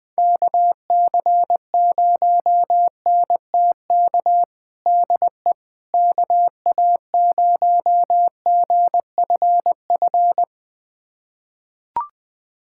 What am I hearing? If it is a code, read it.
Morse code, heard as a keyed tone, about 21 wpm, 700 Hz. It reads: KC0NTK DE KA0GFF
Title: USA Callsign DE Callsign 20wpm